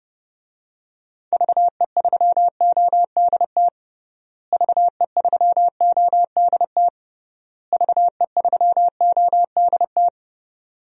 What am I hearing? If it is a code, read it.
Morse code, heard as a keyed tone, about 30 wpm, 700 Hz. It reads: VE3ODT VE3ODT VE3ODT
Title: DXCC-100 Call Signs - 6 Characters - Repeated 3x - 30wpm